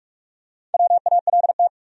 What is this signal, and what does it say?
Morse code, heard as a keyed tone, about 46 wpm, 700 Hz. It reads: WALT
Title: USA Common Names - Max 5 Letters - 45wpm